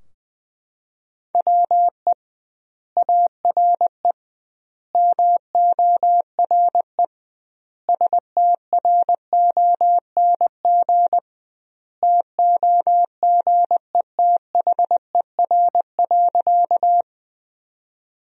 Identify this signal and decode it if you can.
Morse code, heard as a keyed tone, about 21 wpm, 700 Hz. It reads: WE ARE MORE STRONG TOGETHER.
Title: 2x Word-Spacing - Sentences from Top 500 Words - 20wpm